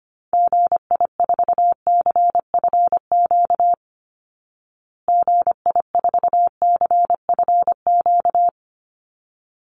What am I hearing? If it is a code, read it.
Morse code, heard as a keyed tone, about 25 wpm, 700 Hz. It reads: GI4CFQ GI4CFQ
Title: Call Signs Repeated 3x - 25wpm